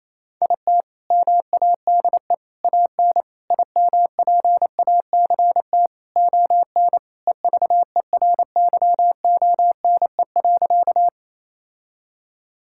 Morse code, heard as a keyed tone, about 28 wpm, 700 Hz. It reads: IT MADE AN IMPACT ON EVERYONE.
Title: Sentences from Top 1000 Words - Review Part 6 - 28wpm